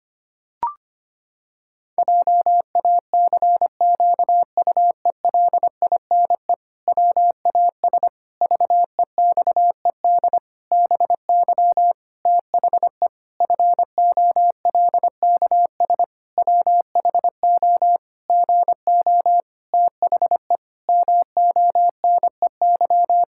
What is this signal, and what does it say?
Morse code, heard as a keyed tone, about 25 wpm, 700 Hz. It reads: JACQUELINE WAS VEXED BY THE FOLKS WHO GO THE MONEY
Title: Pangram Challenge v2 - 25wpm